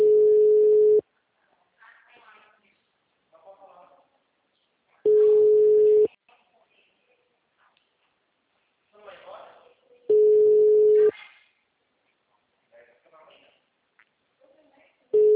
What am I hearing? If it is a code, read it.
Ringback tone (Europe, 425 Hz), with speech.
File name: out- -2008- - - .81595